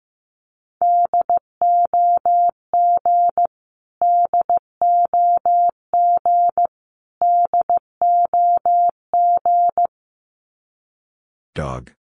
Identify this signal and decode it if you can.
Morse code, heard as a keyed tone, about 15 wpm, 700 Hz. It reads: DOG DOG DOG
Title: Top 500 Words Repeated 15wpm